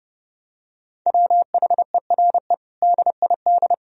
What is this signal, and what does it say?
Morse code, heard as a keyed tone, about 30 wpm, 700 Hz. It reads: WHERE DID
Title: Sentences from Top 1000 Words - Review Part 4 - 30wpm